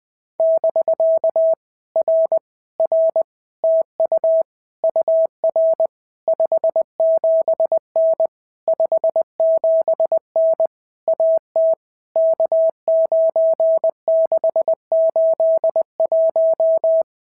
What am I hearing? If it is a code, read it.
Morse code, heard as a keyed tone, about 20 wpm, 650 Hz. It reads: <BK> R R TU UR 57N 57N AT K9681
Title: POTA QSO Practice - 20wpm